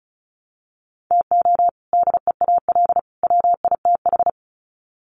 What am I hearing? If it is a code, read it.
Morse code, heard as a keyed tone, about 34 wpm, 700 Hz. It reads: TO DEAL WITH
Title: Sets of 3 Words 35wpm